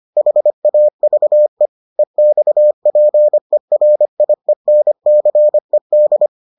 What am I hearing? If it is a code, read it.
Morse code, heard as a keyed tone, about 25 wpm, 600 Hz. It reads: HAVE EXPERIENCED